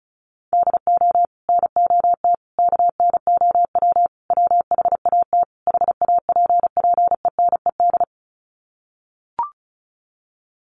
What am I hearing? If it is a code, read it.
Morse code, heard as a keyed tone, about 34 wpm, 700 Hz. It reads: DO NOT KNOW WHAT HAPPENED
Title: Sets of 5 Words Encore2 35wpm